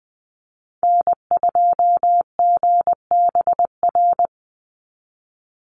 Morse code, heard as a keyed tone, about 20 wpm, 700 Hz. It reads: N2GBR
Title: NA SOTA Chaser Callsigns 20wpm